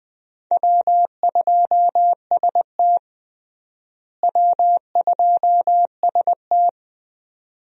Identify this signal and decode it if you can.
Morse code, heard as a keyed tone, about 20 wpm, 700 Hz. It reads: W2ST W2ST
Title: USA Call Signs - 4 Characters - Repeated 2x - 20wpm